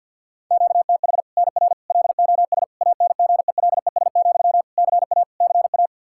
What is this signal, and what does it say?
Morse code, heard as a keyed tone, about 51 wpm, 700 Hz. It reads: QTH NR LOS ANGELES, CA CA
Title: Practice QSO - Exchange 1 - Beginning - 50wpm